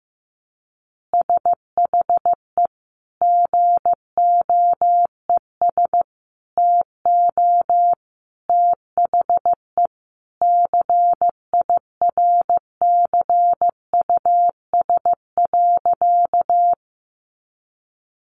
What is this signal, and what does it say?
Morse code, heard as a keyed tone, about 15 wpm, 700 Hz. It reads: SHE GOES TO THE CIRCUS.
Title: Easy Phrases 15wpm